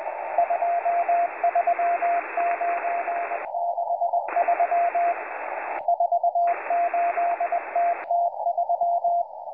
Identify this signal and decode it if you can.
Morse code, heard as a keyed tone, about 20 wpm, 690 Hz, with radio noise. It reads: 23963 48TT3